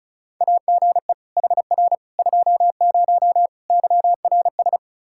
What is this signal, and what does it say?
Morse code, heard as a keyed tone, about 34 wpm, 700 Hz. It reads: AGE HR 20 YRS
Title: Practice QSO - Exchange 3 - More Info - 35wpm